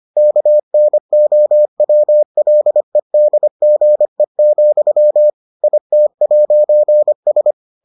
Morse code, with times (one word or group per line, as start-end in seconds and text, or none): 0.2-5.3 KNOWLEDGE,
5.6-7.5 IT'S